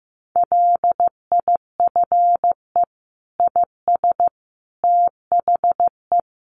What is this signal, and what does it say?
Morse code, heard as a keyed tone, about 15 wpm, 700 Hz. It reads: LIFE IS THE
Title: Sentences from Top 900 Words 15wpm